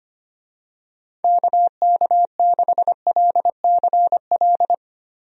Morse code, heard as a keyed tone, about 25 wpm, 700 Hz. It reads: KK6LCL
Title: USA Call Signs - 6 Characters - Repeated 2x - 25wpm